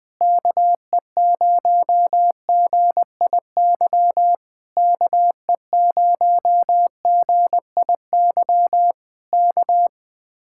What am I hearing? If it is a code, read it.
Morse code, heard as a keyed tone, about 20 wpm, 700 Hz. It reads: KE0GIY KE0GIY K
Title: Speed-Racing Common QSO Phrases - 20wpm